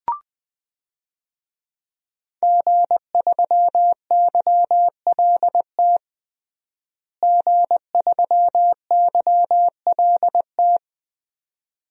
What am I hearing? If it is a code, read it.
Morse code, heard as a keyed tone, about 20 wpm, 700 Hz. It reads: G3YLT G3YLT
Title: DXCC-100 Call Signs - 5 Characters - Repeated 2x - 20wpm